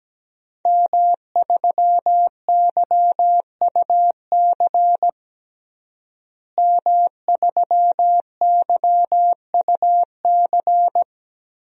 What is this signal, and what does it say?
Morse code, heard as a keyed tone, about 17 wpm, 700 Hz. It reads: M3YUC M3YUC
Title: DXCC-100 Call Signs - 5 Characters - Repeated 3x - 17wpm